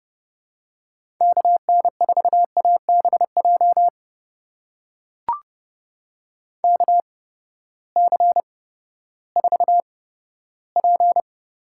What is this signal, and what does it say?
Morse code, heard as a keyed tone, about 31 wpm, spaced as for 15 wpm, 700 Hz. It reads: KN4ABJ KC4P
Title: USA Call Signs - 6 Characters - 8x Character Spacing - 30wpm